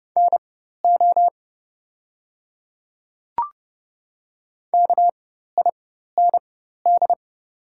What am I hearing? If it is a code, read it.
Morse code, heard as a keyed tone, about 32 wpm, spaced as for 14 wpm, 700 Hz. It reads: NO KIND
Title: 4x Character Spacing - Top 200 Words - 30wpm